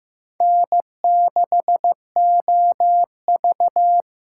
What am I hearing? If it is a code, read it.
Morse code, heard as a keyed tone, about 15 wpm, 700 Hz. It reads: N6OV
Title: USA Call Signs - 6x Character Spacing - 15wpm